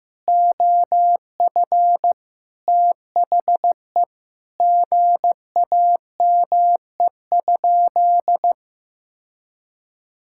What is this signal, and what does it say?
Morse code, heard as a keyed tone, about 15 wpm, 700 Hz. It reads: OF THE GAME?
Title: Sentences from Top 1000 Words - Review Part 5 - 15wpm